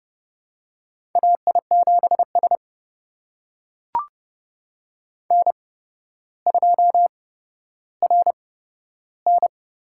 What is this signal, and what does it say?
Morse code, heard as a keyed tone, about 30 wpm, spaced as for 13 wpm, 700 Hz. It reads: AI7S N2RN
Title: USA Call Signs - 8x Character Spacing - 30wpm